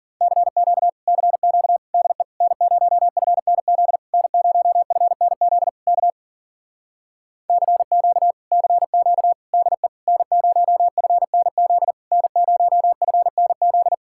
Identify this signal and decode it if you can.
Morse code, heard as a keyed tone, about 50 wpm, 700 Hz. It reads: CQ CQ DE N0FNZ N0FNZ K CQ CQ DE N0FNZ N0FNZ